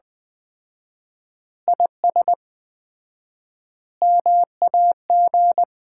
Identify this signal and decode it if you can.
Morse code, heard as a keyed tone, about 20 wpm, 700 Hz. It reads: IS MAG
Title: Common QSO Phrases - 4x Word-Spacing - 20wpm